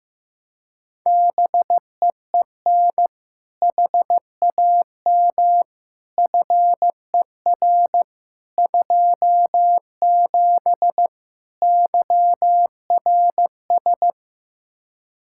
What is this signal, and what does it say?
Morse code, heard as a keyed tone, about 15 wpm, 700 Hz. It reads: BEEN HAM FER 27 YRS
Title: Practice QSO - Exchange 3 - More Info - 15wpm